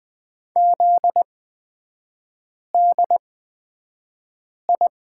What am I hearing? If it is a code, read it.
Morse code, heard as a keyed tone, about 20 wpm, spaced as for 5 wpm, 700 Hz. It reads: ZDI
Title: 243 Lesson 39e - Letter z - Three Characters 20wpm